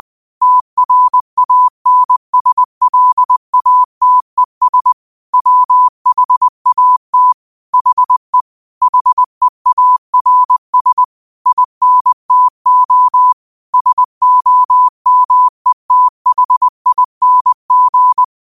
Morse code, heard as a keyed tone, about 20 wpm, 1000 Hz. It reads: TRANSLATES WHAT HE HEARS INTO SOMETHING
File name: qotdfiles / 20wpm